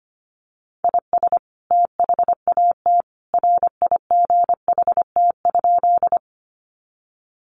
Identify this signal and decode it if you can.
Morse code, heard as a keyed tone, about 25 wpm, 700 Hz. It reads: IS THAT RIGHT?
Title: Sentences from Top 200 Words 25wpm